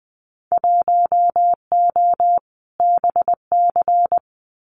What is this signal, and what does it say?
Morse code, heard as a keyed tone, about 20 wpm, 700 Hz. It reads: 1O BC
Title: Winter Field Day Exchanges Encore 20wpm